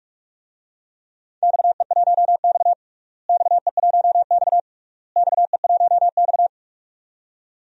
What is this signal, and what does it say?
Morse code, heard as a keyed tone, about 46 wpm, 700 Hz. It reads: XE1X XE1X XE1X
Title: DXCC-100 Call Signs - 4 Characters - Repeated 3x - 45wpm